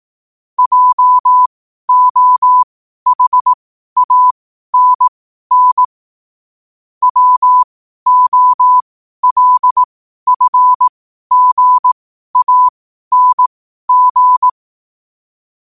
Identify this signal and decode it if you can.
Morse code, heard as a keyed tone, about 18 wpm, spaced as for 12 wpm, 1000 Hz. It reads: JOHANN WOLFGANG
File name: qotdfiles / 12wpm